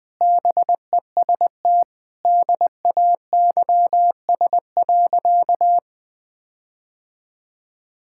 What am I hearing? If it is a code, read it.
Morse code, heard as a keyed tone, about 21 wpm, 700 Hz. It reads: BEST DAYS.
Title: Sentences with 5-Word Phrases - Spoken Beginning - 20wpm